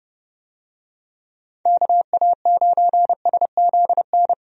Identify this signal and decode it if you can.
Morse code, heard as a keyed tone, about 30 wpm, 700 Hz. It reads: KA9SZN
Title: USA Call Signs - 6 Characters - Repeated 2x - 30wpm